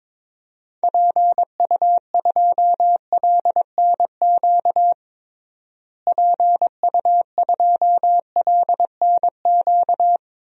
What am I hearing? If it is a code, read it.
Morse code, heard as a keyed tone, about 21 wpm, 700 Hz. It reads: PU2LNQ PU2LNQ
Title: DXCC-100 Call Signs - 6 Characters - Repeated 2x - 22wpm